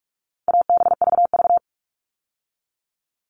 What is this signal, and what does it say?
Morse code, heard as a keyed tone, about 44 wpm, 700 Hz. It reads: ABVV
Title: All Cut Numbers - 4-Digit 45wpm